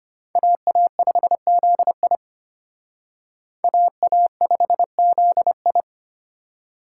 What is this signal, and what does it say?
Morse code, heard as a keyed tone, about 32 wpm, 700 Hz. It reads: AA5ZI AA5ZI